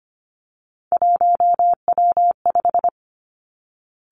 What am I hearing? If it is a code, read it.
Morse code, heard as a keyed tone, about 25 wpm, 700 Hz. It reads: 1W5
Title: Three Challenging Letter-Number Combinations 25wpm